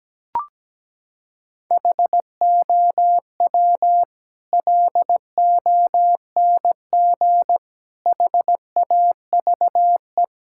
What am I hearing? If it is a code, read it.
Morse code, heard as a keyed tone, about 17 wpm, 700 Hz. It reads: HOW LONG HAVE
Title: Sentences from Top 1000 Words - Review Part 6 - 17wpm